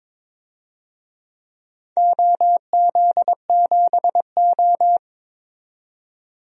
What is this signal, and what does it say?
Morse code, heard as a keyed tone, about 22 wpm, 700 Hz. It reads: OZ7O
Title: DXCC-100 Call Signs - 4 Characters - 22wpm